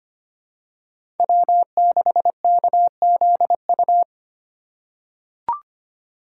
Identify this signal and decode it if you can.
Morse code, heard as a keyed tone, about 25 wpm, 700 Hz. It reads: W6KZU
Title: USA Call Signs - 5 Characters - 4x Character Spacing - 25wpm